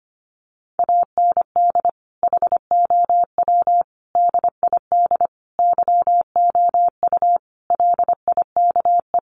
Morse code, heard as a keyed tone, about 25 wpm, 700 Hz. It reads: AND HOW DID YOU LIKE
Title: Sentences from Top 1300 Words 25wpm